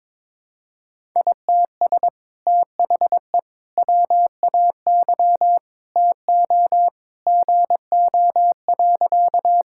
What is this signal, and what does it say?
Morse code, heard as a keyed tone, about 21 wpm, 700 Hz. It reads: ITS THE WAY TO GO.
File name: practice / 22wpm / Sentences with 2-Word Phrases - 22wpm